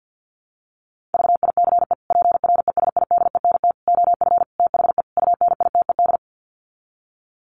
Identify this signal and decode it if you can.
Morse code, heard as a keyed tone, about 50 wpm, 700 Hz. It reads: VICE PRESIDENT OF THE UNITED
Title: Sets of 5 Words Encore2 50wpm